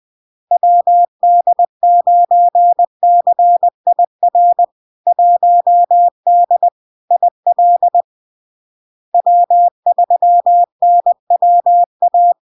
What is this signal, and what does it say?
Morse code, heard as a keyed tone, about 20 wpm, 700 Hz. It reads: WD9CIR 1D IL W3NWA
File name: LARK-FD-2022-Log-in-CW-20wpm-700-Hz